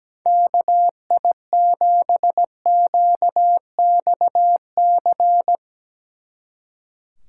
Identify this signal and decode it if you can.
Morse code, heard as a keyed tone, about 17 wpm, 700 Hz. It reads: KI7QXC
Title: Speed Racing - USA 2x3 Call Signs 17wpm